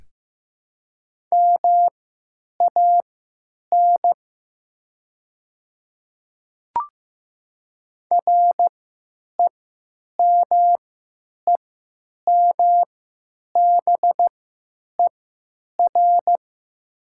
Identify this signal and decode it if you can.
Morse code, heard as a keyed tone, about 15 wpm, spaced as for 9 wpm, 700 Hz. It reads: MAN REMEMBER
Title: 3x Character Spacing - Top 500 Words - 15wpm